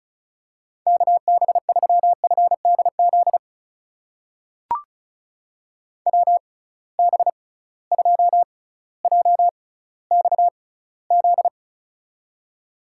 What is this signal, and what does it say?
Morse code, heard as a keyed tone, about 36 wpm, spaced as for 18 wpm, 700 Hz. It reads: KB3FDZ WB2JXZ
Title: USA Call Signs - 6 Characters - 6x Character Spacing - 35wpm